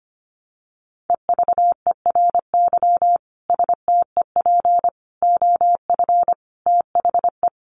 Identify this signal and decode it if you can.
Morse code, heard as a keyed tone, about 25 wpm, 700 Hz. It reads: EVERY STEP OF THE
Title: Sets of 5 Words Encore2 25wpm